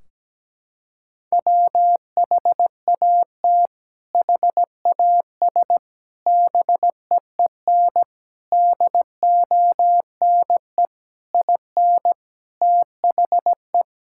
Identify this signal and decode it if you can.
Morse code, heard as a keyed tone, about 17 wpm, 700 Hz. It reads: WHAT HAS BEEN DONE IN THE
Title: Sentences from Top 1600 Words 17wpm